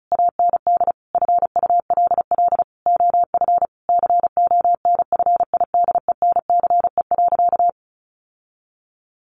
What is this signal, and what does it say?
Morse code, heard as a keyed tone, about 34 wpm, 700 Hz. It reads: AND FULL OF CONFIDENCE.
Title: Sentences from Top 1500 Words 35wpm